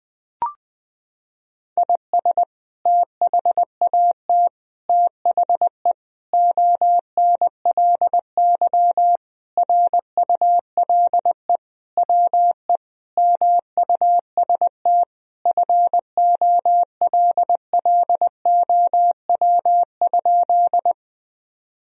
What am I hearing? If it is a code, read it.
Morse code, heard as a keyed tone, about 20 wpm, 700 Hz. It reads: IS THAT THE ONLY RULE WE MUST FOLLOW?
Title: Sentences from Top 1000 Words - Review Part 2 - 20wpm